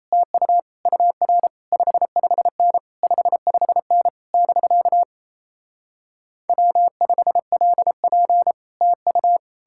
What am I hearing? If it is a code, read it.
Morse code, heard as a keyed tone, about 38 wpm, 700 Hz. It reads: TU UR 55N 55N <BK> W5LP TU